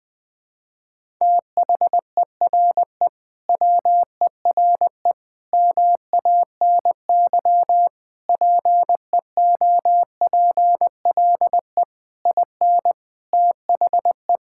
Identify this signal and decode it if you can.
Morse code, heard as a keyed tone, about 21 wpm, 700 Hz. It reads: THERE WERE MANY PEOPLE IN THE
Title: Sentences with 3-Word Phrases - Spoken Beginning - 20wpm